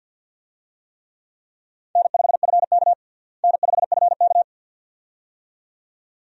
Morse code, heard as a keyed tone, about 53 wpm, 700 Hz. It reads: N5FK N5FK
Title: USA Call Signs - 4 Characters - Repeated 2x - 50wpm